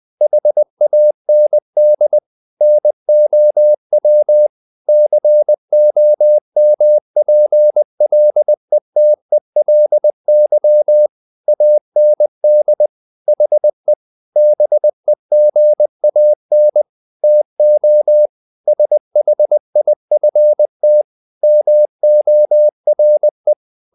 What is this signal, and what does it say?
Morse code, heard as a keyed tone, about 20 wpm, 600 Hz. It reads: HAND NOW COMPLETELY AND HE BEGAN TO SHIFT MORE
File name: the old man and the sea / the old man and the sea part 1 20wpm